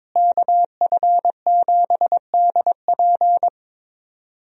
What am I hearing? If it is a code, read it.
Morse code, heard as a keyed tone, about 22 wpm, 700 Hz. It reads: KF7DP
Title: USA Call Signs - 5 Characters - Repeated 2x - 22wpm